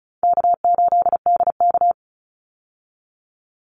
Morse code, heard as a keyed tone, about 34 wpm, 700 Hz. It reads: K8DK
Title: Easy Call Signs Repeated 3x - 35wpm